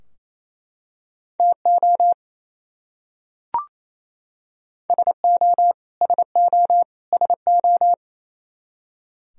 Morse code, heard as a keyed tone, about 28 wpm, 700 Hz. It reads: TO SO SO SO
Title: Top 100 Words Repeated 28wpm